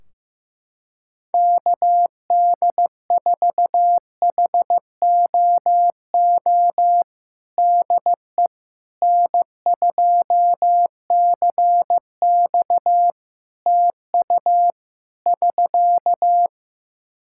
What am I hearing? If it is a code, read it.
Morse code, heard as a keyed tone, about 15 wpm, 700 Hz. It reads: KD4HOO DE N2CX TU <SK>
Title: Practice QSO - Exchange 4 - Ending - 15wpm